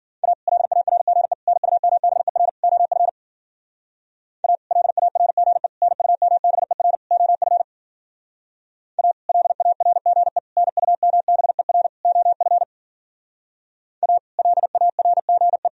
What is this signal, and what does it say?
Morse code, heard as a keyed tone, about 52 wpm, 700 Hz. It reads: A LARGE NUMBER OF A LARGE NUMBER OF A LARGE NUMBER OF A LARGE